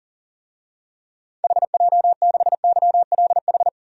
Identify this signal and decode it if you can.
Morse code, heard as a keyed tone, about 37 wpm, 700 Hz. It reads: HJ6YLH
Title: DXCC-100 Call Signs - 6 Characters - 40wpm